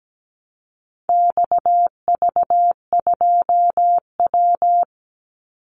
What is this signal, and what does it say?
Morse code, heard as a keyed tone, about 17 wpm, 700 Hz. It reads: XV2W
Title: Call Signs Repeated 3x - 17wpm